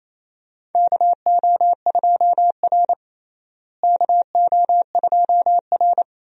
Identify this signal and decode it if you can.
Morse code, heard as a keyed tone, about 28 wpm, 700 Hz. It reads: KO2R KO2R
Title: USA Call Signs - 4 Characters - Repeated 3x - 28wpm